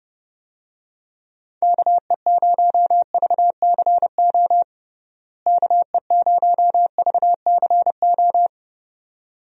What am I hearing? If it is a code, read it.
Morse code, heard as a keyed tone, about 32 wpm, 700 Hz. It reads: KE0VCO KE0VCO
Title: USA Call Signs - 6 Characters - Repeated 2x - 30wpm